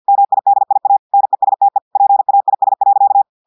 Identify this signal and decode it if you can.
Morse code, heard as a keyed tone, about 50 wpm, 800 Hz. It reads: MIDIA NESTE PAIS.
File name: audios / portugais / portugais texte en clair 1 vitesse 50 mots minutes